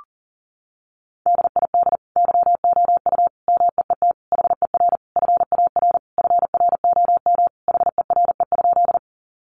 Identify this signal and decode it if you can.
Morse code, heard as a keyed tone, about 40 wpm, 700 Hz. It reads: DID YOU MEET HER FAR FROM HERE?
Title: Sentences from Top 1100 Words 40wpm